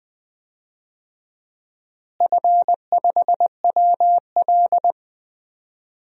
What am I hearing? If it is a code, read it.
Morse code, heard as a keyed tone, about 20 wpm, 700 Hz. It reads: F5WL
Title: DXCC-100 Call Signs - 4 Characters - Repeated 2x - 20wpm